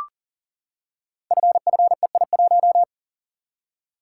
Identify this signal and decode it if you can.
Morse code, heard as a keyed tone, about 38 wpm, 690 Hz. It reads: FFEI1
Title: Code Groups - 5 Letter:Numbers - 40wpm